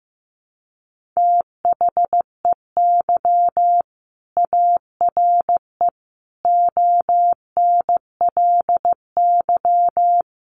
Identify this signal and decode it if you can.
Morse code, heard as a keyed tone, about 15 wpm, 700 Hz. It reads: THEY ARE ONLY
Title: Sentences from Top 1400 Words 15wpm